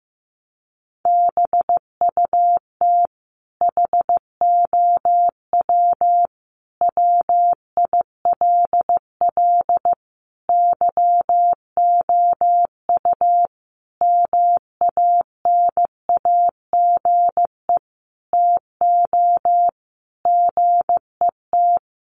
Morse code, heard as a keyed tone, about 15 wpm, 700 Hz. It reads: BUT HOW WILL YOU MANAGE TO GET
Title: Sentences from Top 1400 Words 15wpm